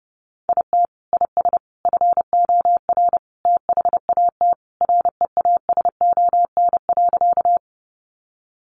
Morse code, heard as a keyed tone, about 28 wpm, 700 Hz. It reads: IT IS FOR THAT REASON.
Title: Sentences from Top 700 Words 30wpm